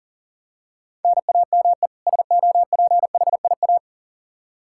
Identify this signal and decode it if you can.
Morse code, heard as a keyed tone, about 40 wpm, 700 Hz. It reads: NAME SOPHIA
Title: Practice QSO - Exchange 1 - Beginning - 40wpm